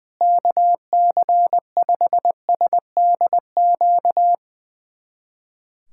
Morse code, heard as a keyed tone, about 20 wpm, 700 Hz. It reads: KC5SDQ
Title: USA Callsign DE Callsign - 6 Characters - 20wpm